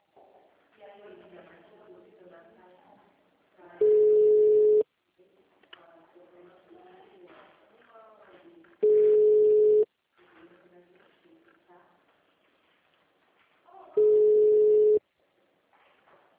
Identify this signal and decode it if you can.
Ringback tone (Europe, 425 Hz), with speech.
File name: out- -2010- - - .112